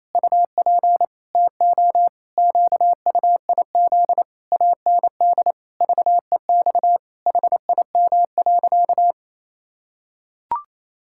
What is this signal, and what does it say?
Morse code, heard as a keyed tone, about 27 wpm, 700 Hz. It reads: UP TO QUIZ AND VEX HIM.
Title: Pangram Challenge v2 - 28wpm